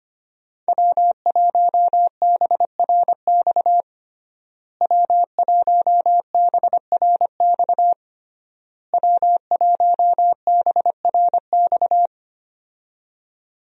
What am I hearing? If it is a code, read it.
Morse code, heard as a keyed tone, about 25 wpm, 700 Hz. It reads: W1BRX W1BRX W1BRX
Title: USA Call Signs - 5 Characters - Repeated 3x - 25wpm